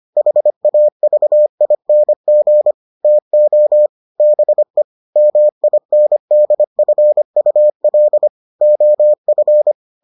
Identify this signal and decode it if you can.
Morse code, heard as a keyed tone, about 25 wpm, 600 Hz. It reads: HAVING TO BE MINDFUL OF